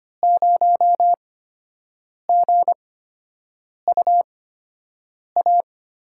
Morse code, heard as a keyed tone, about 25 wpm, spaced as for 7 wpm, 700 Hz. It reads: 0GUA
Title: USA Call Signs - 6 Characters - 8x Character Spacing - 25wpm